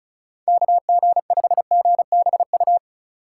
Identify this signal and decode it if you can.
Morse code, heard as a keyed tone, about 36 wpm, 700 Hz. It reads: KG5GBU
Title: USA Call Signs - 2x Character Spacing - 35wpm